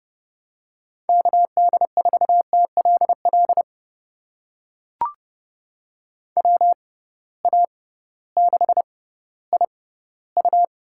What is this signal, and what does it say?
Morse code, heard as a keyed tone, about 31 wpm, spaced as for 16 wpm, 700 Hz. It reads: KD4TLL WA6IU
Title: USA Call Signs - 6 Characters - 6x Character Spacing - 30wpm